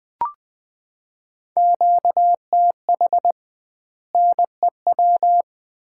Morse code, heard as a keyed tone, about 20 wpm, 700 Hz. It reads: QTH NEW
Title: Common QSO Phrases - 2x Word-Spacing - 20wpm